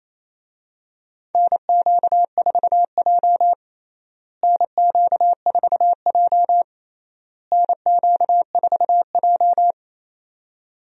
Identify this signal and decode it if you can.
Morse code, heard as a keyed tone, about 28 wpm, 700 Hz. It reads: NQ4J NQ4J NQ4J
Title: USA Call Signs - 4 Characters - Repeated 3x - 28wpm